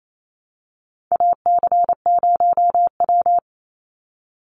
Morse code, heard as a keyed tone, about 28 wpm, 700 Hz. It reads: AC0W
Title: Call Signs Repeated 2x - 28wpm